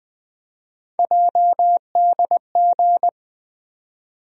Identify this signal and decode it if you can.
Morse code, heard as a keyed tone, about 21 wpm, 700 Hz. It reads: JDG
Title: USA Call Sign Postfixes - 3 Letters 20wpm